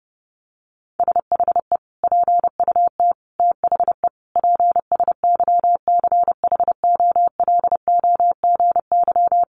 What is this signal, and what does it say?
Morse code, heard as a keyed tone, about 30 wpm, 700 Hz. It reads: SHE PUT THE PSYCHOLOGY
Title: Sentences from Top 1400 Words 30wpm